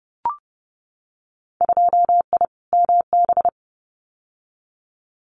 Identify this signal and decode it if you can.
Morse code, heard as a keyed tone, about 30 wpm, 700 Hz. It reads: 2I MB
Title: Winter Field Day Exchanges Encore 30wpm